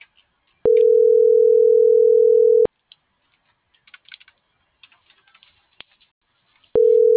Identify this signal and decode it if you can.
Ringback tone (North America), 440 + 480 Hz.